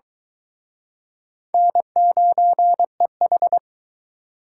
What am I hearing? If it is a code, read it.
Morse code, heard as a keyed tone, about 23 wpm, 700 Hz. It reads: N9EH